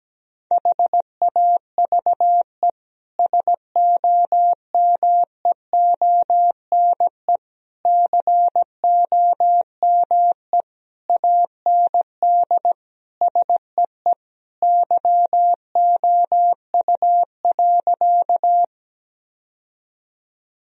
Morse code, heard as a keyed tone, about 17 wpm, 700 Hz. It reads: HAVE SOMEONE COME AND SEE YOU.
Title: Sentences from Top 1000 Words - Review Part 4 - 17wpm